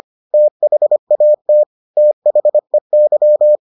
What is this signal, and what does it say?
Morse code, heard as a keyed tone, about 25 wpm, 600 Hz. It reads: THAT THEY